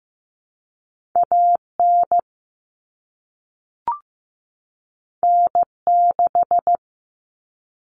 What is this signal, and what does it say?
Morse code, heard as a keyed tone, about 15 wpm, 700 Hz. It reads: A N N 6
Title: Common Cut Numbers - 2-Digit 15wpm